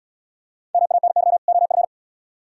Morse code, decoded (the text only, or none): MANY OF